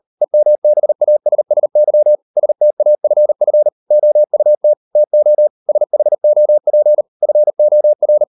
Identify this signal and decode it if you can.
Morse code, heard as a keyed tone, about 38 wpm, 600 Hz. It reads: EMBASSY STAFF OUT TO SHOP FOR